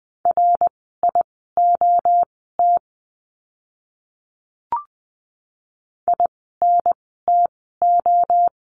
Morse code, heard as a keyed tone, about 20 wpm, spaced as for 14 wpm, 700 Hz. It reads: RIOT INTO
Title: 061 Lesson 11f - Letter h - Words 20wpm